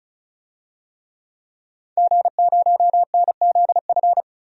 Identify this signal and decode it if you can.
Morse code, heard as a keyed tone, about 35 wpm, 700 Hz. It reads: G0NZF
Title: DXCC-100 Call Signs - 5 Characters - 35wpm